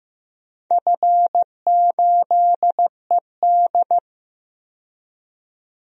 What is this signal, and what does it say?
Morse code, heard as a keyed tone, about 15 wpm, 700 Hz. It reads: F8ED
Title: DXCC-100 Call Signs - 4 Characters - 15wpm